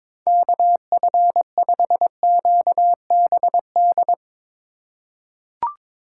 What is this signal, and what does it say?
Morse code, heard as a keyed tone, about 22 wpm, 700 Hz. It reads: KF5QBD
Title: USA Call Signs - 3x Character Spacing - 22wpm